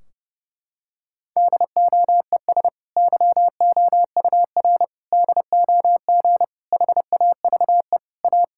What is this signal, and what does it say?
Morse code, heard as a keyed tone, about 31 wpm, 700 Hz. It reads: DOES YOUR DOG HAVE A
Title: Sentences from Top 1600 Words 30wpm